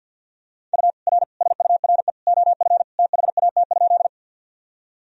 Morse code, heard as a keyed tone, about 52 wpm, 700 Hz. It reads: U R SURE OF THAT?
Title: Sentences from Top 300 Words and Abbreviations 50wpm